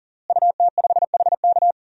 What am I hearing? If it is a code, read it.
Morse code, heard as a keyed tone, about 40 wpm, 700 Hz. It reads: UT5HK
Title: DXCC-100 Call Signs - 5 Characters - Repeated 2x - 40wpm